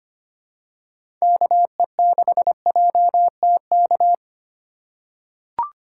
Morse code, heard as a keyed tone, about 25 wpm, 700 Hz. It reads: KE6JTK
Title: USA Call Signs - 4x Character Spacing - 25wpm